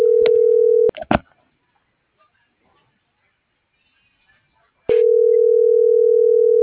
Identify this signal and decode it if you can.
Ringback tone (North America), 440 + 480 Hz.